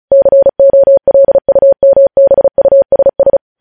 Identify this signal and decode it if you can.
Morse code, heard as a keyed tone, about 36 wpm, 550 Hz. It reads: COLUMBUSS